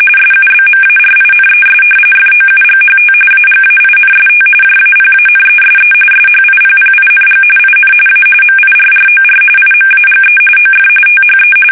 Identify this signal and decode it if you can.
RTTY, about 45.45 baud. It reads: SUBMARINES FOUGHT AGAINST JAPANESE SHIPPING IN A MIRROR IMAGE OF THE
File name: fsk850